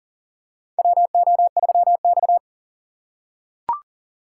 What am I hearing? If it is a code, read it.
Morse code, heard as a keyed tone, about 40 wpm, 700 Hz. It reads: WO3X
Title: USA Call Signs - 4 Characters - 6x Character Spacing - 40wpm